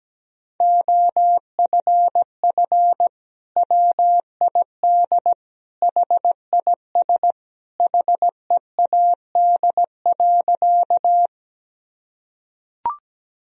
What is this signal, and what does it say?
Morse code, heard as a keyed tone, about 17 wpm, 700 Hz. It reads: OFF WID HIS HEAD.
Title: Sentences from Top 300 Words and Abbreviations 17wpm